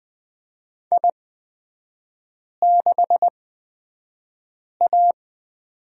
Morse code, heard as a keyed tone, about 20 wpm, spaced as for 5 wpm, 700 Hz. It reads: I6A
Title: 255 Lesson 41e - Period - Three Characters 20wpm